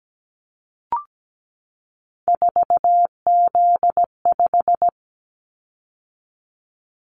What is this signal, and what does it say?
Morse code, heard as a keyed tone, about 17 wpm, 700 Hz. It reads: 4Z5
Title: Callsign Prefixes for Top 50 DXCC - 17wpm